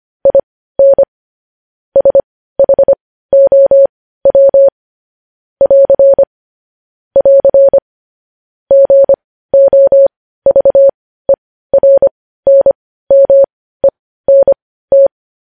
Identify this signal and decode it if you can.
Morse code, heard as a keyed tone, about 25 wpm, spaced as for 15 wpm, 550 Hz. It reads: IN SHOW <AR> <AR> GOVERNMENT
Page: news headlines in morse code at 15 wpm